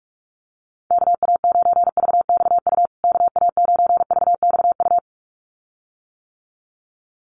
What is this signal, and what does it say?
Morse code, heard as a keyed tone, about 44 wpm, 700 Hz. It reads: KA9VXU KA9VXU
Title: USA Call Signs Repeated 2x 45wpm